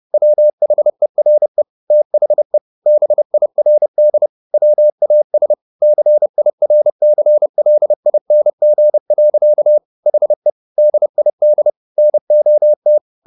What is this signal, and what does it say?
Morse code, heard as a keyed tone, about 30 wpm, 600 Hz. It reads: WHERE THE BIRD WAS CIRCLING. HE DID NOT
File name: the old man and the sea / the old man and the sea 1 30wpm